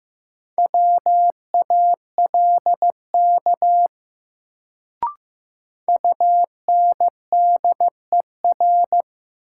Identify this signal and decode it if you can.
Morse code, heard as a keyed tone, about 15 wpm, 700 Hz. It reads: WALK UNDER
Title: Sentences from Top 300 Words - New Word Drill - 15wpm